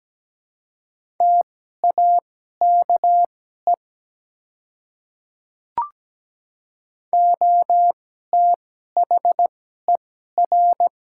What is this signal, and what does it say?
Morse code, heard as a keyed tone, about 17 wpm, spaced as for 12 wpm, 700 Hz. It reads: TAKE OTHER
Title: 2x Character Spacing - Top 200 Words - 17wpm